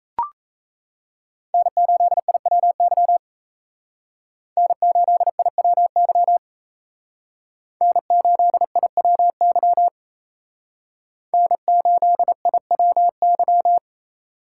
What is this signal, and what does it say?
Morse code, heard as a keyed tone, about 35 wpm, 700 Hz. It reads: N8IWY N8IWY N8IWY N8IWY
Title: Speed Racing - USA 1x3 Call Signs 28wpm